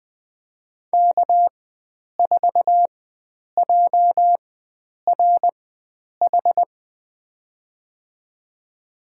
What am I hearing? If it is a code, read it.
Morse code, heard as a keyed tone, about 20 wpm, spaced as for 9 wpm, 700 Hz. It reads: K4JRH
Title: USA Call Signs - 4x Character Spacing - 20wpm